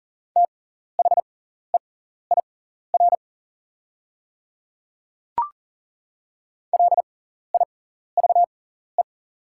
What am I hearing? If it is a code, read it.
Morse code, heard as a keyed tone, about 40 wpm, spaced as for 14 wpm, 690 Hz. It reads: THEIR LIVE
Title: 6x Character Spacing - Top 200 Words - 40wpm